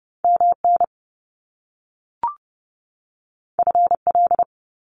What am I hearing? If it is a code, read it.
Morse code, heard as a keyed tone, about 30 wpm, 700 Hz. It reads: MN FL
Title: US States 30wpm